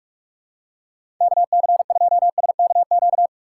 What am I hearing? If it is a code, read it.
Morse code, heard as a keyed tone, about 46 wpm, 700 Hz. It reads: KC2SKQ
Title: USA Call Signs - 6 Characters - 6x Character Spacing - 45wpm